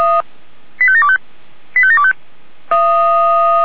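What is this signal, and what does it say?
Five-tone selective call (ZVEI), heard as quick runs of five tones, tones of about 70 ms. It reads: 85725 85725
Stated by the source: ZVEI five-tone selective call received on SDR.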